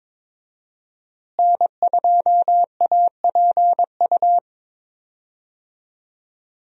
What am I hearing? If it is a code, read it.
Morse code, heard as a keyed tone, about 22 wpm, 700 Hz. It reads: N2APU
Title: USA Call Signs - 5 Characters - Repeated 2x - 22wpm